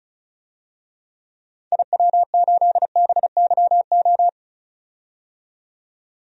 Morse code, heard as a keyed tone, about 34 wpm, 700 Hz. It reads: IW8BYO
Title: DXCC-100 Call Signs - 6 Characters - 35wpm